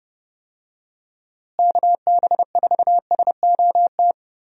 Morse code, heard as a keyed tone, about 32 wpm, 700 Hz. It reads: KB4SOT
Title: USA Call Signs - 6 Characters - Repeated 2x - 30wpm